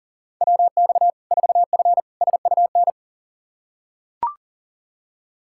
Morse code, heard as a keyed tone, about 40 wpm, 700 Hz. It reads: WX 4F SUN
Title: Practice QSO - Exchange 2 - Info - 40wpm